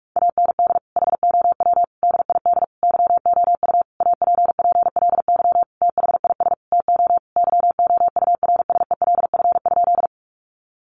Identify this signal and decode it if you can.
Morse code, heard as a keyed tone, about 44 wpm, 700 Hz. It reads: AND HOW DID YOU APPLY THIS TO YOURSELF?
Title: Sentences from Top 1200 Words 45wpm